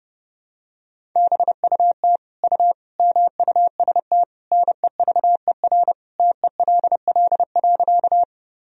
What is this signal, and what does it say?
Morse code, heard as a keyed tone, about 30 wpm, 700 Hz. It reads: BUT U MUST NEVER TELL.
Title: Sentences from Top 300 Words and Abbreviations 30wpm